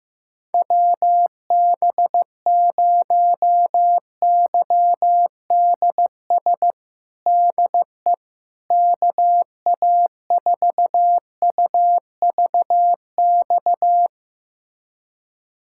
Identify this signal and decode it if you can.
Morse code, heard as a keyed tone, about 16 wpm, 700 Hz. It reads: WB0YDS DE KA4UVX
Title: USA Callsign DE Callsign - 6 Characters - 15wpm